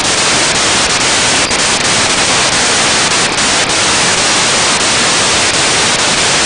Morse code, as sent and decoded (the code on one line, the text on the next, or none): none